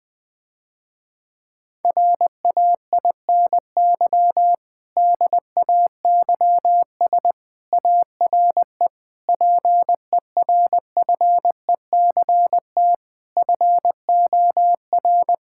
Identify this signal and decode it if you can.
Morse code, heard as a keyed tone, about 20 wpm, 700 Hz. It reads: RAINY DAYS ARE PERFECT FOR
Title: Difficult Sentences with Hints - 20wpm